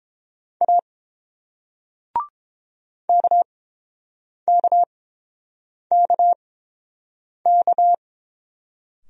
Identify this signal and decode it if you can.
Morse code, heard as a keyed tone, about 31 wpm, spaced as for 7 wpm, 700 Hz. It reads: A KKKK